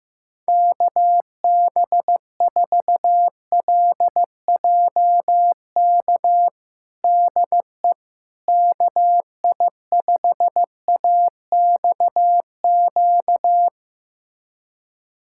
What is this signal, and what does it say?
Morse code, heard as a keyed tone, about 15 wpm, 700 Hz. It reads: KB4LJK DE KI5AXQ
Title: USA Callsign DE Callsign - 6 Characters - 15wpm